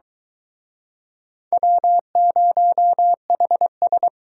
Morse code, heard as a keyed tone, about 25 wpm, 700 Hz. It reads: W0HS